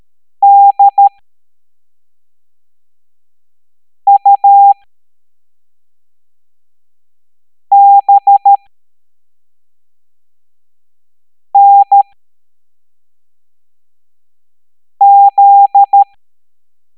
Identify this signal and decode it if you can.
Morse code, heard as a keyed tone, about 13 wpm, spaced as for 3 wpm, 800 Hz. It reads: DUBNZ